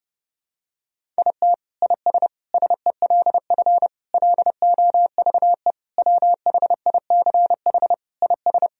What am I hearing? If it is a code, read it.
Morse code, heard as a keyed tone, about 32 wpm, 700 Hz. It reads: IT IS SELF LOVE WHICH IS
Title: Sentences from Top 1000 Words - Review Part 1 - 30wpm